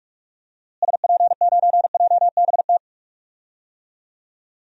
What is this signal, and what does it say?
Morse code, heard as a keyed tone, about 42 wpm, 700 Hz. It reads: SP9JBT
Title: DXCC-100 Call Signs - 6 Characters - Repeated 3x - 45wpm